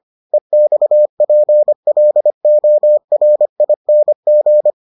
Morse code, heard as a keyed tone, about 25 wpm, 600 Hz. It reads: EXPLORING